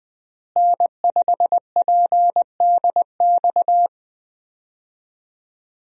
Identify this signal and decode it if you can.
Morse code, heard as a keyed tone, about 20 wpm, 700 Hz. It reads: N5PDX
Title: USA Call Signs - 5 Characters - Repeated 3x - 20wpm